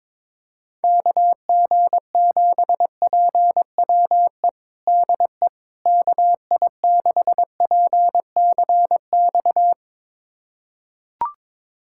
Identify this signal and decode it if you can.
Morse code, heard as a keyed tone, about 22 wpm, 700 Hz. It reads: KG7PWE DE KI6PCX
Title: USA Callsign DE Callsign - 6 Characters - 22wpm